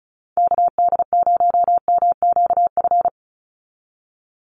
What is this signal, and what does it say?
Morse code, heard as a keyed tone, about 34 wpm, 700 Hz. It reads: KD0MQF
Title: USA Call Signs 35wpm